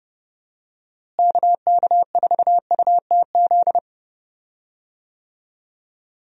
Morse code, heard as a keyed tone, about 30 wpm, 700 Hz. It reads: KK4UTZ
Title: USA Call Signs - 6 Characters - Repeated 2x - 30wpm